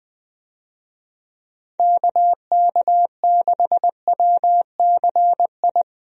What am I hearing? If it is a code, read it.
Morse code, heard as a keyed tone, about 20 wpm, 700 Hz. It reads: KK6WCI
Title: USA Call Signs - 6 Characters - Repeated 2x - 20wpm